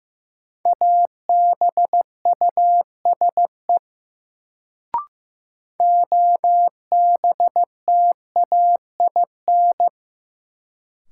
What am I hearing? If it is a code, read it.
Morse code, heard as a keyed tone, about 15 wpm, 700 Hz. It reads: ABUSE OBTAIN
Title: Sentences from Top 1300 Words - New Word Drill Alt - 15wpm